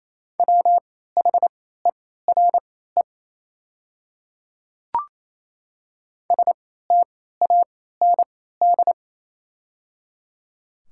Morse code, heard as a keyed tone, about 27 wpm, spaced as for 16 wpm, 700 Hz. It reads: WHERE STAND
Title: 3x Character Spacing - Top 300 Words - 28wpm